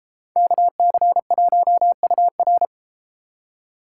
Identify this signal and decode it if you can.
Morse code, heard as a keyed tone, about 32 wpm, 700 Hz. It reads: KC1UR